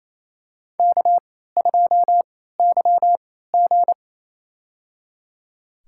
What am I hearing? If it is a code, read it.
Morse code, heard as a keyed tone, about 28 wpm, spaced as for 16 wpm, 700 Hz. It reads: K2YG
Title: USA Call Signs - 4 Characters - 3x Character Spacing - 28wpm